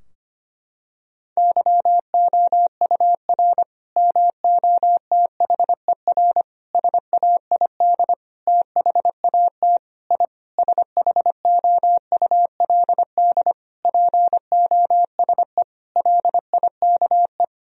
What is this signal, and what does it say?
Morse code, heard as a keyed tone, about 25 wpm, 700 Hz. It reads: YOUR MOTHER SAID THAT I SHOULD POSE LIKE
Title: Sentences from Top 1000 Words - Review Part 2 - 25wpm